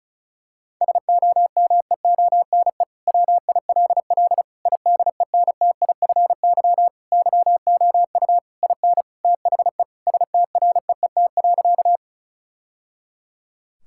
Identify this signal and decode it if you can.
Morse code, heard as a keyed tone, about 34 wpm, 690 Hz. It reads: SOMEONE WILL IDENTIFY YOU IN THE STREET.
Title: Sentences from Top 1000 Words - Review Part 6 - 35wpm